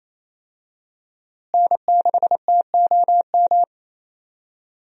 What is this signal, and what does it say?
Morse code, heard as a keyed tone, about 28 wpm, 700 Hz. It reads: N6TOM
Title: USA Call Signs - 5 Characters - Repeated 2x - 28wpm